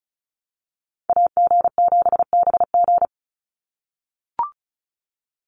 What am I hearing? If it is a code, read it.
Morse code, heard as a keyed tone, about 34 wpm, 700 Hz. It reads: AG7BG
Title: USA Call Signs 35wpm